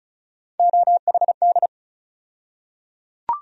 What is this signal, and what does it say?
Morse code, heard as a keyed tone, about 36 wpm, 700 Hz. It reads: OHD
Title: USA Call Sign Postfixes - 3 Letters 35wpm